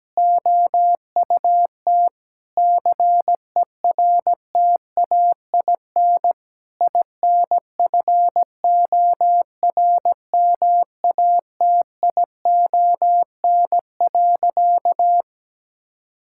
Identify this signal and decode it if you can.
Morse code, heard as a keyed tone, about 17 wpm, 700 Hz. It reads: OUT CERTAIN INFORMATION.
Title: Sentences from Top 1000 Words - Review Part 6 - 17wpm